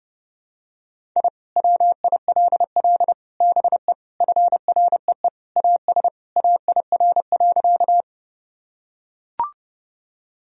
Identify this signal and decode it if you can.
Morse code, heard as a keyed tone, about 30 wpm, 700 Hz. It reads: I WILL BE FREE AS AIR.
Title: Sentences from Top 1000 Words - Review Part 1 - 30wpm